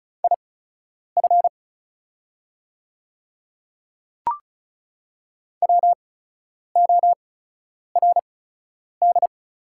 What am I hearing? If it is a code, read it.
Morse code, heard as a keyed tone, about 36 wpm, spaced as for 10 wpm, 690 Hz. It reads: IF WORD
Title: 8x Character Spacing - Top 200 Words - 35wpm